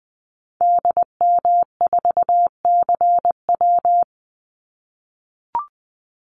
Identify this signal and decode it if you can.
Morse code, heard as a keyed tone, about 20 wpm, 700 Hz. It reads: DM4CW
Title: Call Signs - 20wpm